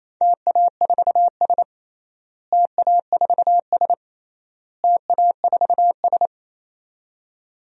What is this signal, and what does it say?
Morse code, heard as a keyed tone, about 28 wpm, 700 Hz. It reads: TA4S TA4S TA4S
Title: DXCC-100 Call Signs - 4 Characters - Repeated 3x - 28wpm